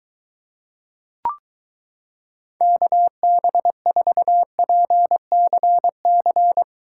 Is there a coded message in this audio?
Yes: Morse code from 2.6 to 6.6 s, heard as a keyed tone, about 22 wpm, 700 Hz. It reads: KB4PCC